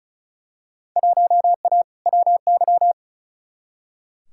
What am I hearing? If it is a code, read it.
Morse code, heard as a keyed tone, about 34 wpm, 700 Hz. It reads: 1A WY
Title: Field Day Exchanges - 35wpm